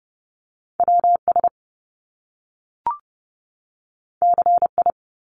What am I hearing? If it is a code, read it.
Morse code, heard as a keyed tone, about 30 wpm, 700 Hz. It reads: WS CI
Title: Top Two Letter Combinations in Words 30wpm